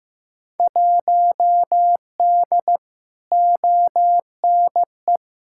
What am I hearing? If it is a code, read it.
Morse code, heard as a keyed tone, about 16 wpm, 700 Hz. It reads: 1D ONE
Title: Field Day Exchanges - 15wpm